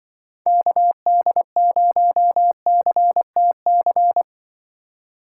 Morse code, heard as a keyed tone, about 23 wpm, 700 Hz. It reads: KD0CTC
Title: Speed Racing - USA 2x3 Call Signs 20wpm